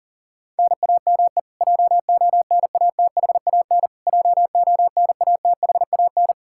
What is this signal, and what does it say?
Morse code, heard as a keyed tone, about 40 wpm, 700 Hz. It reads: NAME JONATHAN JONATHAN
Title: Practice QSO - Exchange 1 - Beginning - 40wpm